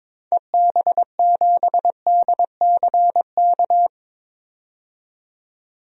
Morse code, heard as a keyed tone, about 22 wpm, 700 Hz. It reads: EB7DCK
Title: DXCC-100 Call Signs - 6 Characters - 22wpm